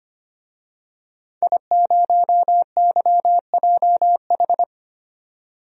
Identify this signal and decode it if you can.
Morse code, heard as a keyed tone, about 25 wpm, 700 Hz. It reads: I0YJH
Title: DXCC-100 Call Signs - 5 Characters - 25wpm